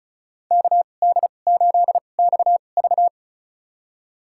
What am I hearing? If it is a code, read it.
Morse code, heard as a keyed tone, about 36 wpm, spaced as for 25 wpm, 700 Hz. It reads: KD8XV
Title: USA Call Signs - 5 Characters - 2x Character Spacing - 35wpm